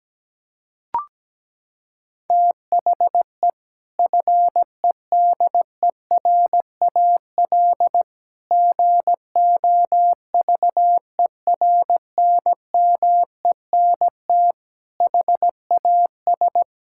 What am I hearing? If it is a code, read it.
Morse code, heard as a keyed tone, about 17 wpm, 700 Hz. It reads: THE FEDERAL GOVERNMENT HAS
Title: Sentences from Top 1000 Words - Review Part 5 - 17wpm